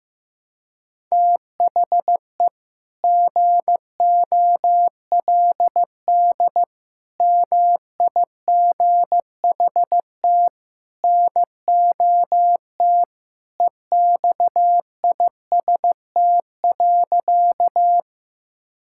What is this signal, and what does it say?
Morse code, heard as a keyed tone, about 15 wpm, 700 Hz. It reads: THE GOLD MIGHT NOT EXIST.
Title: Sentences from Top 1000 Words - Review Part 5 - 15wpm